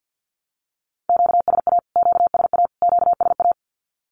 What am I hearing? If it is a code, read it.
Morse code, heard as a keyed tone, about 48 wpm, 700 Hz. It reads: QSA QSA QSA
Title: Q-Codes Repeated 3x 50wpm